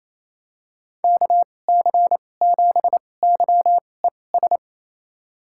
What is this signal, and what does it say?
Morse code, heard as a keyed tone, about 27 wpm, spaced as for 20 wpm, 700 Hz. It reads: KC7YES
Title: USA Call Signs - 6 Characters - 2x Character Spacing - 28wpm